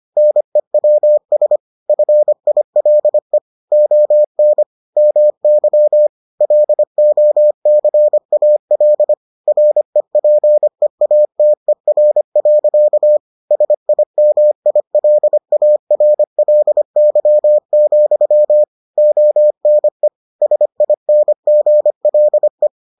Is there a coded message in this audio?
Yes: Morse code from 0.2 to 22.7 s, heard as a keyed tone, about 25 wpm, 600 Hz. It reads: NEWS FILE ON MY LOCAL REPEATER. SIMILARLY, ONE SINGLE